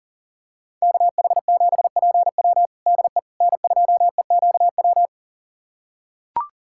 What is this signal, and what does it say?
Morse code, heard as a keyed tone, about 40 wpm, 700 Hz. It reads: KH7PW DE N2EQW
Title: USA Callsign DE Callsign - 5 Characters - 40wpm